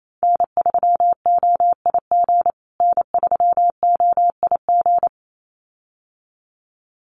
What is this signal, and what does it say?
Morse code, heard as a keyed tone, about 28 wpm, 700 Hz. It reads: N3OIG N3OIG
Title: USA Call Signs Repeated 2x 28wpm